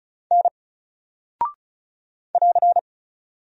Morse code, heard as a keyed tone, about 36 wpm, 700 Hz. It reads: N <AR>
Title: Single Character (Letter:Number:Punctuation:Prosign) Alt 35wpm